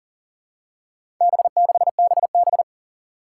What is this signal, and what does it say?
Morse code, heard as a keyed tone, about 40 wpm, 700 Hz. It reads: B6BB
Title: Four Challenging Letter-Number Combinations 40wpm